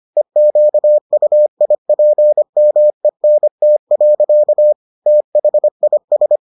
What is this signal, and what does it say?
Morse code, heard as a keyed tone, about 25 wpm, 600 Hz. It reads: EQUIPMENT. THIS